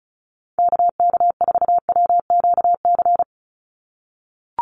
Morse code, heard as a keyed tone, about 34 wpm, 700 Hz. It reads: KK4WQC
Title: USA Call Signs 35wpm